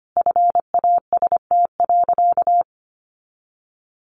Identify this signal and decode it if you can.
Morse code, heard as a keyed tone, about 25 wpm, 700 Hz. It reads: FAST.
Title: Sentences from Top 1300 Words 25wpm